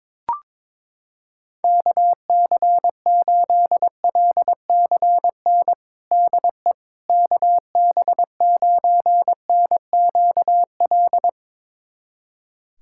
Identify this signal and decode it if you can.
Morse code, heard as a keyed tone, about 22 wpm, 700 Hz. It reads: KC8LCN DE KB9NQL
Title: USA Callsign DE Callsign - 6 Characters - 22wpm